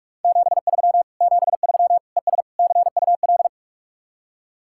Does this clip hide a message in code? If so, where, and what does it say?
Yes: Morse code from 0.2 to 3.5 s, heard as a keyed tone, about 44 wpm, 700 Hz. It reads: 73 73 ES CUL